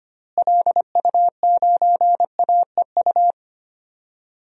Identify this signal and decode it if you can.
Morse code, heard as a keyed tone, about 25 wpm, 700 Hz. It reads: LU9AEU
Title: DXCC-100 Call Signs - 6 Characters - Repeated 2x - 25wpm